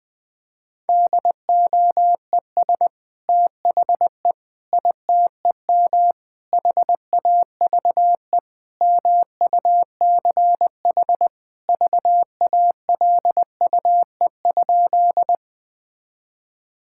Morse code, heard as a keyed tone, about 20 wpm, 700 Hz. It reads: DOES THE ITEM HAVE MUCH VALUE?
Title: Sentences from Top 1000 Words - Review Part 6 - 20wpm